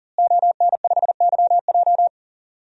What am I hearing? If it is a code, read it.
Morse code, heard as a keyed tone, about 38 wpm, 700 Hz. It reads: ON5YJ
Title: DXCC-100 Call Signs - 5 Characters - Repeated 3x - 40wpm